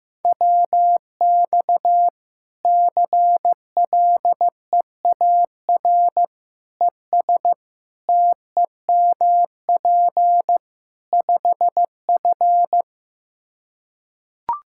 Morse code, heard as a keyed tone, about 15 wpm, 700 Hz. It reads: WX CLEAR ES TEMP 5F
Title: Practice QSO - Exchange 2 - Info - 15wpm